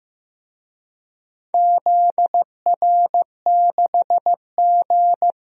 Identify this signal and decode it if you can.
Morse code, heard as a keyed tone, about 15 wpm, 700 Hz. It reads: ZR6G
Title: DXCC-100 Call Signs - 4 Characters - 15wpm